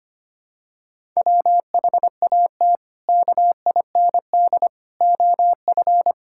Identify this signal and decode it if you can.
Morse code, heard as a keyed tone, about 25 wpm, 700 Hz. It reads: WHAT KIND OF
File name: practice / 25wpm / Sentences with 3-Word Phrases - 25wpm